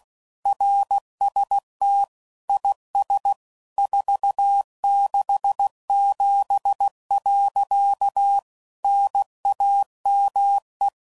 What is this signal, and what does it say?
Morse code, heard as a keyed tone, about 16 wpm, 800 Hz. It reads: RST IS 467. NAME
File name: CW Code Groups @15WPM / QSOs / kc7sac